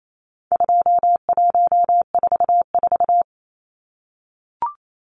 Morse code, heard as a keyed tone, about 28 wpm, 700 Hz. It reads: 2144
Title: Numbers - 4-Digit 28wpm